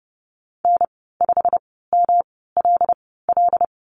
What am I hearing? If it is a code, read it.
Morse code, heard as a keyed tone, about 30 wpm, spaced as for 17 wpm, 700 Hz. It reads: N5MLL
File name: Lesson 19g - Letter w - Call Signs 30wpm